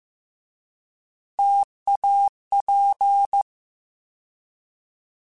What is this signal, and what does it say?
Morse code, heard as a keyed tone, about 15 wpm, 800 Hz. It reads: TAP
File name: Alphabetical / Alphabetical mp3s / Ts @13